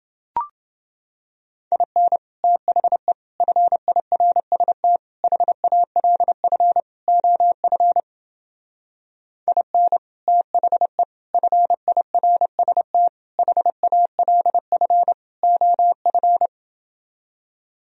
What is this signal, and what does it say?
Morse code, heard as a keyed tone, about 29 wpm, 700 Hz. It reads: IN THE FIRST HALF OF IN THE FIRST HALF OF